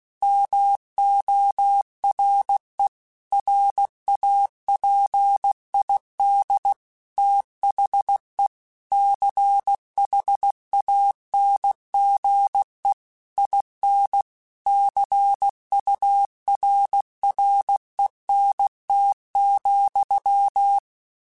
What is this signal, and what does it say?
Morse code, heard as a keyed tone, about 16 wpm, 800 Hz. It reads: MORE RAPID THE CHANGE IN CURRENT,
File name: Technical Subjects / Technical Subjects-mp3s / XL @15